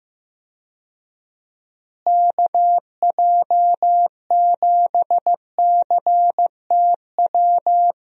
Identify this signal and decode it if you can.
Morse code, heard as a keyed tone, about 15 wpm, 700 Hz. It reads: KJ7CTW
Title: USA Call Signs - 6 Characters - Repeated 3x - 15wpm